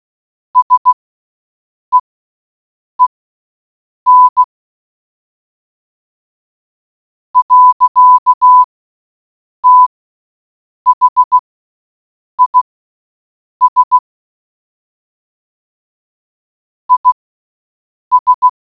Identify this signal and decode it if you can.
Morse code, heard as a keyed tone, about 16 wpm, spaced as for 6 wpm, 1000 Hz. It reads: SEEN. THIS IS